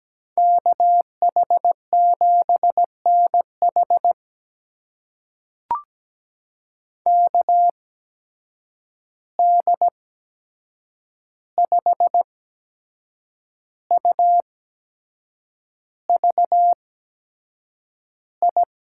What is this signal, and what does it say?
Morse code, heard as a keyed tone, about 17 wpm, spaced as for 7 wpm, 700 Hz. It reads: KH7NH KD5UVI
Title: USA Call Signs - 8x Character Spacing - 17wpm